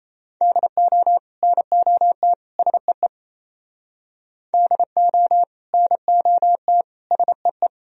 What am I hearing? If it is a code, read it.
Morse code, heard as a keyed tone, about 37 wpm, 700 Hz. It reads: DO NOT SEE DO NOT SEE